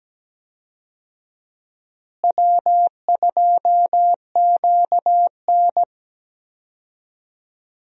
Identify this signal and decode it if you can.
Morse code, heard as a keyed tone, about 17 wpm, 700 Hz. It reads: W2QN
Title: USA Call Signs - 4 Characters - Repeated 3x - 17wpm